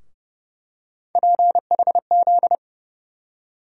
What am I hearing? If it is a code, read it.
Morse code, heard as a keyed tone, about 30 wpm, 700 Hz. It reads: PHZ
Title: USA Call Sign Postfixes - 3 Letters 30wpm